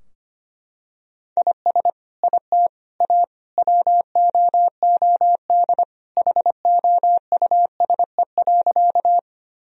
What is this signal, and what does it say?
Morse code, heard as a keyed tone, about 25 wpm, 700 Hz. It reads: IS IT A WOOD HOUSE.
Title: Sentences from Top 1000 Words - Review Part 2 - 25wpm